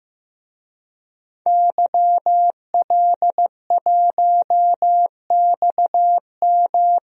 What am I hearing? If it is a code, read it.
Morse code, heard as a keyed tone, about 15 wpm, 700 Hz. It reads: YL1XM
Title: DXCC-100 Call Signs - 5 Characters - Repeated 2x - 15wpm